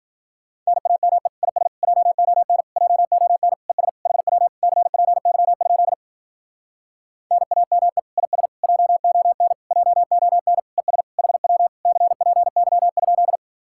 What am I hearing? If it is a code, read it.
Morse code, heard as a keyed tone, about 51 wpm, 700 Hz. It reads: NAME IS JON JON ES HW CPY? NAME IS JON JON ES HW CPY?